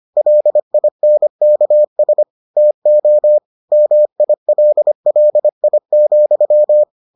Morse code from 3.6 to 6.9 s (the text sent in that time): MILLI,